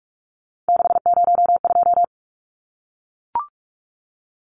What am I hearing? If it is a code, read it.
Morse code, heard as a keyed tone, about 44 wpm, 700 Hz. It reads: 602
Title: Numbers - 3-Digit 45wpm